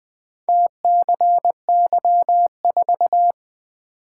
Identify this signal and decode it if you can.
Morse code, heard as a keyed tone, about 20 wpm, 700 Hz. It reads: TCY4
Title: Four Letter-Number Combinations - 20wpm